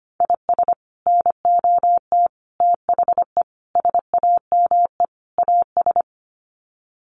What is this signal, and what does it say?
Morse code, heard as a keyed tone, about 25 wpm, 700 Hz. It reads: IS NOT THE SAME AS
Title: Sets of 5 Words Encore 25wpm